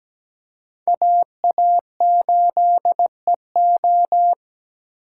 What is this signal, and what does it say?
Morse code, heard as a keyed tone, about 20 wpm, 700 Hz. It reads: AA8EO
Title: Speed Racing - USA 2x2 Call Signs 17wpm